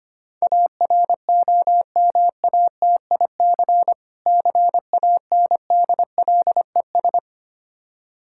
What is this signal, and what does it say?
Morse code, heard as a keyed tone, about 25 wpm, 700 Hz. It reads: AROMATIC CANDLES
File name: practice / 25wpm / Difficult Words with Hints - 25wpm